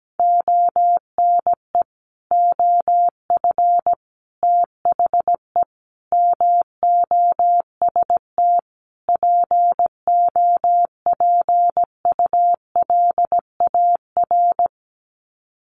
Morse code, heard as a keyed tone, about 17 wpm, 700 Hz. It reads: ONE OF THE MOST POPULAR
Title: Sets of 5 Words 17wpm